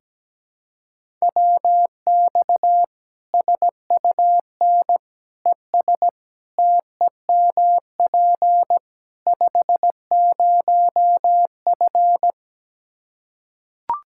Morse code, heard as a keyed tone, about 17 wpm, 700 Hz. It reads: WX SUN ES TEMP 50F
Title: Practice QSO - Exchange 2 - Info - 17wpm